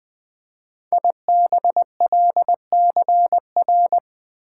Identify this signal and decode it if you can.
Morse code, heard as a keyed tone, about 20 wpm, 700 Hz. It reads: IBLCR
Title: Code Groups - 5 Letter:Numbers - 20wpm